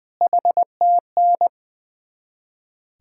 Morse code, heard as a keyed tone, about 20 wpm, 700 Hz. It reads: HTN
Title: Ontario QSO Party Exchange 20wpm